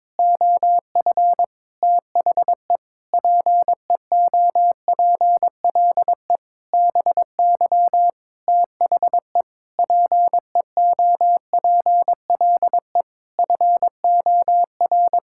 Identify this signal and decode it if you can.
Morse code, heard as a keyed tone, about 22 wpm, 700 Hz. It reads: OF THE PEOPLE BY THE PEOPLE FOR
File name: Sets of 9 Words - 22wpm